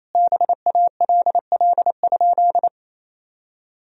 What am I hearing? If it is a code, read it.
Morse code, heard as a keyed tone, about 28 wpm, 700 Hz. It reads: BALL?
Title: Sentences from Top 1000 Words - Review Part 3 - 28wpm